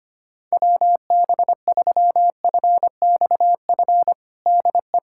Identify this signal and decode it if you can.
Morse code, heard as a keyed tone, about 25 wpm, 700 Hz. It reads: WB3FXF DE
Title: USA Callsign DE Callsign 25wpm